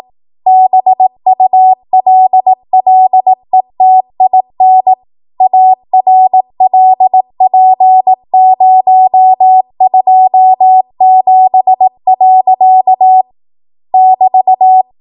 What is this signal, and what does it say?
Morse code, heard as a keyed tone, about 18 wpm, 750 Hz. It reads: BULLETIN ARLP027. <BT>
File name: Morse / 18WPM